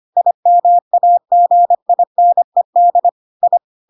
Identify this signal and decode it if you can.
Morse code, heard as a keyed tone, about 25 wpm, 700 Hz. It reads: IMAGINED I
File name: Pinocchio / Pinocchio ch1 25 25 ik1hge